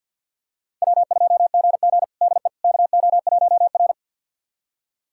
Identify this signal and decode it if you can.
Morse code, heard as a keyed tone, about 52 wpm, 700 Hz. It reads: W2GG DE KO1R
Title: USA Callsign DE Callsign - 4 Characters - 50wpm